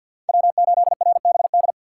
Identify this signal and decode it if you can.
Morse code, heard as a keyed tone, about 52 wpm, 700 Hz. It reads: W8RBD
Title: USA Call Signs - 5 Characters - 8x Character Spacing - 50wpm